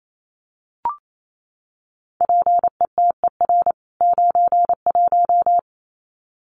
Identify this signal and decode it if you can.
Morse code, heard as a keyed tone, about 28 wpm, 700 Hz. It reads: PETER 91
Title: CWT Exchange 28wpm